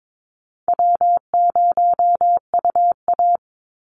Morse code, heard as a keyed tone, about 22 wpm, 700 Hz. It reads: W0UA
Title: Easy Call Signs Repeated 3x - 22wpm